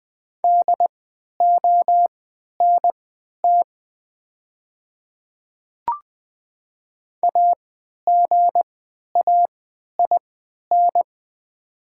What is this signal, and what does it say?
Morse code, heard as a keyed tone, about 20 wpm, spaced as for 11 wpm, 690 Hz. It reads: DONT AGAIN
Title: 3x Character Spacing - Top 300 Words - 20wpm